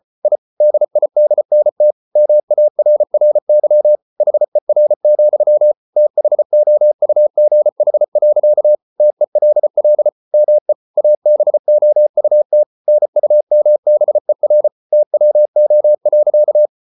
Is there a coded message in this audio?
Yes: Morse code from 0.2 to 16.7 s, heard as a keyed tone, about 34 wpm, 600 Hz. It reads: I DIDNT MARRY HER, THOUGH. TELL ME ABOUT NUMBER TWO.